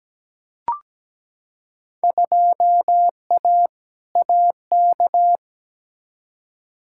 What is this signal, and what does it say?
Morse code, heard as a keyed tone, about 17 wpm, 700 Hz. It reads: 2A AK
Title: Field Day Exchanges Encore - 17wpm